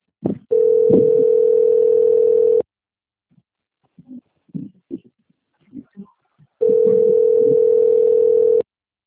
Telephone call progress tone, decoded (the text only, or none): ringback tone